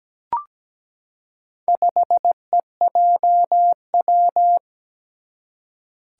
Morse code, heard as a keyed tone, about 17 wpm, 700 Hz. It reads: 5EJW
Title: Four Challenging Letter-Number Combinations 17wpm